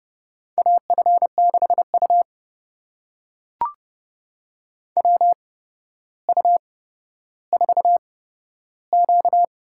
Morse code, heard as a keyed tone, about 30 wpm, spaced as for 13 wpm, 700 Hz. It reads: AF6U WU4Q
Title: USA Call Signs - 4 Characters - 8x Character Spacing - 30wpm